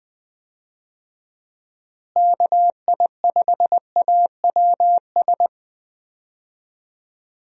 Morse code, heard as a keyed tone, about 20 wpm, 700 Hz. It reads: KI5AWS
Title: USA Call Signs - 6 Characters - Repeated 2x - 20wpm